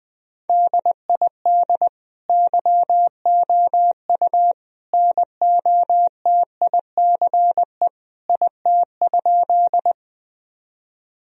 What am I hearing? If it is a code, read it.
Morse code, heard as a keyed tone, about 20 wpm, 700 Hz. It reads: DID YOU NOTICE IT?
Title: Sentences from Top 1000 Words - Review Part 2 - 20wpm